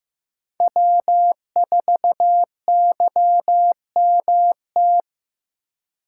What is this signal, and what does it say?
Morse code, heard as a keyed tone, about 15 wpm, 700 Hz. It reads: W4YMT
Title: USA Call Signs - 5 Characters - 2x Character Spacing - 15wpm